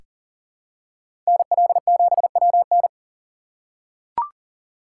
Morse code, heard as a keyed tone, about 38 wpm, 700 Hz. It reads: NL7WN
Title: USA Call Signs - 3x Character Spacing - 40wpm